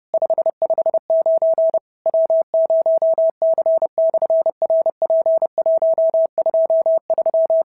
Morse code, heard as a keyed tone, about 30 wpm, 650 Hz. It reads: 559 W0C/RP123
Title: SOTA QSO Practice 30wpm